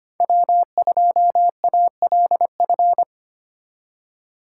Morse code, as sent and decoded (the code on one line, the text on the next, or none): .-- ..--- .- .-.. ..-.
W2ALF